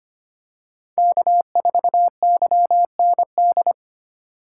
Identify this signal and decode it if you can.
Morse code, heard as a keyed tone, about 25 wpm, 700 Hz. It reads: K4YND
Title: K1USN SST QSO Practice - 25wpm